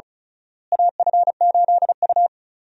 Morse code, heard as a keyed tone, about 36 wpm, 700 Hz. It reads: AF8U
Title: USA Call Signs - 4 Characters - Repeated 2x - 35wpm